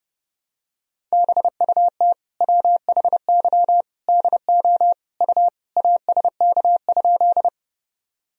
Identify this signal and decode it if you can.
Morse code, heard as a keyed tone, about 30 wpm, 700 Hz. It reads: BUT WHY DO U ASK?
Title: Sentences from Top 200 Words and Abbreviations 30wpm